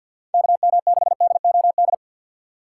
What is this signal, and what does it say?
Morse code, heard as a keyed tone, about 52 wpm, 700 Hz. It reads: KM6DOD
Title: USA Call Signs - 6 Characters - 3x Character Spacing - 50wpm